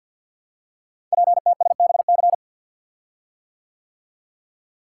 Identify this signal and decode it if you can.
Morse code, heard as a keyed tone, about 48 wpm, 700 Hz. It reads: PTSBC
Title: Code Groups - 5 Letters - 50wpm